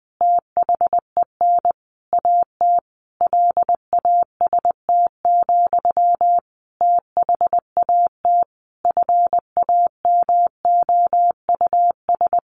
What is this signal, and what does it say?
Morse code, heard as a keyed tone, about 20 wpm, 700 Hz. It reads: THEN AT LAST, THAT FAMOUS
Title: Sentences from Top 1500 Words 20wpm